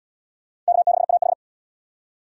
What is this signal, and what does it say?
Morse code, heard as a keyed tone, about 67 wpm, 700 Hz. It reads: K6AH